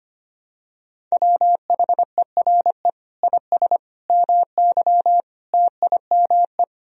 Morse code, heard as a keyed tone, about 25 wpm, 700 Hz. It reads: WHERE IS MY TIME
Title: Sentences from Top 1000 Words - Review Part 2 - 25wpm